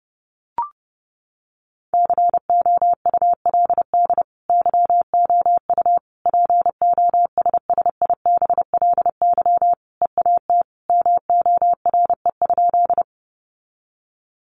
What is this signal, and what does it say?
Morse code, heard as a keyed tone, about 30 wpm, 700 Hz. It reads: COULD YOU POSSIBLY EAT MORE?
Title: Sentences from Top 1300 Words 30wpm